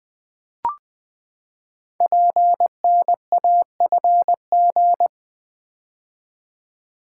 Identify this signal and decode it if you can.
Morse code, heard as a keyed tone, about 20 wpm, 700 Hz. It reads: PNAFG
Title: Code Groups - 5 Letters - 20wpm